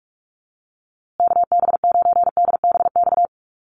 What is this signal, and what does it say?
Morse code, heard as a keyed tone, about 44 wpm, 700 Hz. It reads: KB9DBX
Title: USA Call Signs Repeated 3x 45wpm